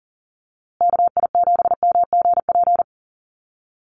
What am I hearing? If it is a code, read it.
Morse code, heard as a keyed tone, about 40 wpm, 700 Hz. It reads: KI7MGP
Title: USA Call Signs Repeated 2x 40wpm